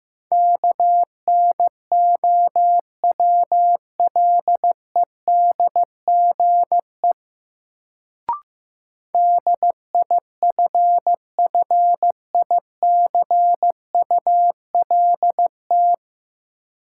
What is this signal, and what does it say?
Morse code, heard as a keyed tone, about 15 wpm, 700 Hz. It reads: KNOWLEDGE DIFFICULT
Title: Sentences from Top 600 Words - New Word Drill - 15wpm